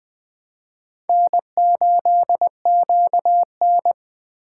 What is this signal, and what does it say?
Morse code, heard as a keyed tone, about 21 wpm, 700 Hz. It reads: N8QN